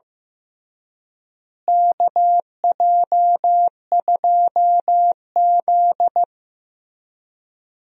Morse code, heard as a keyed tone, about 15 wpm, 700 Hz. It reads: KJ2Z
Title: USA Call Signs - 4 Characters - Repeated 3x - 15wpm